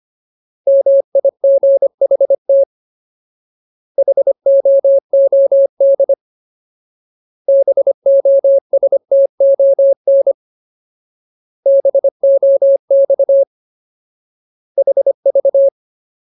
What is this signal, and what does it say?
Morse code, heard as a keyed tone, about 25 wpm, 550 Hz. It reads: MIGHT HOOD BOSTON BOX HV
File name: Ultimate Word List / 25WPM Ultimate